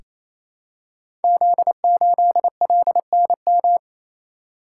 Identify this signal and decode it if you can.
Morse code, heard as a keyed tone, about 27 wpm, 700 Hz. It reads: Z8LNM
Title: Code Groups - 5 Letter:Numbers - 28wpm